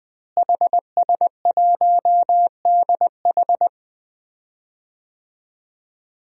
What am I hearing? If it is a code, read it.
Morse code, heard as a keyed tone, about 20 wpm, 700 Hz. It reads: HS1DH
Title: DXCC-100 Call Signs - 5 Characters - Repeated 2x - 20wpm